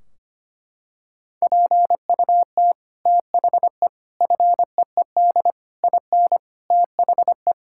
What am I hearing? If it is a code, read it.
Morse code, heard as a keyed tone, about 25 wpm, 700 Hz. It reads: PUT THE FEED IN THE
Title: Sentences from Top 1600 Words 25wpm